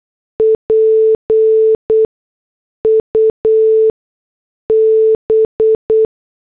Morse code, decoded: PUB